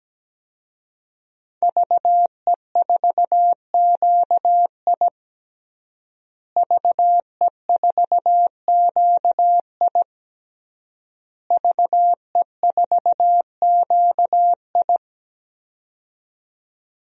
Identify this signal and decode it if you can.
Morse code, heard as a keyed tone, about 17 wpm, 700 Hz. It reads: VE4QI VE4QI VE4QI
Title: DXCC-100 Call Signs - 5 Characters - Repeated 3x - 17wpm